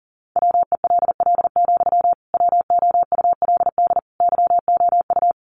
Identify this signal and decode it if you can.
Morse code, heard as a keyed tone, about 40 wpm, 700 Hz. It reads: WELL, WOULD YOU
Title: Sentences from Top 1000 Words 40wpm